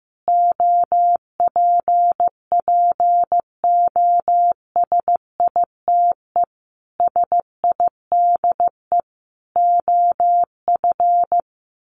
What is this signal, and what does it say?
Morse code, heard as a keyed tone, about 14 wpm, 700 Hz. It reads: OPPOSITE SIDE OF
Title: Sets of 5 Words Encore2 15wpm